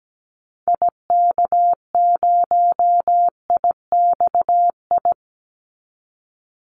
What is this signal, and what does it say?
Morse code, heard as a keyed tone, about 17 wpm, 700 Hz. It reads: IK0IXI
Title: Call Signs Repeated 3x - 17wpm